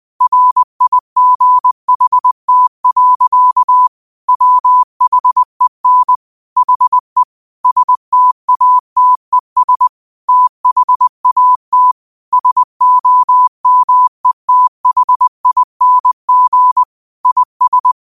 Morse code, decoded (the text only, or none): RIGHT. WHEN HE STATES THAT SOMETHING IS